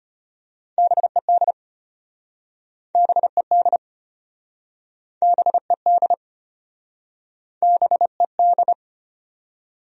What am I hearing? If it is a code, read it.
Morse code, heard as a keyed tone, about 35 wpm, 700 Hz. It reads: BED BED BED BED